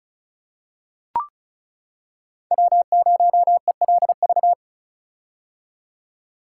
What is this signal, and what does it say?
Morse code, heard as a keyed tone, about 36 wpm, 700 Hz. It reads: W0ELV
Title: Code Groups - 5 Letter:Numbers - 35wpm